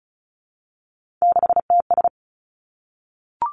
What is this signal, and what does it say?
Morse code, heard as a keyed tone, about 36 wpm, 700 Hz. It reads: 6TS
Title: Three Challenging Letter-Number Combinations 35wpm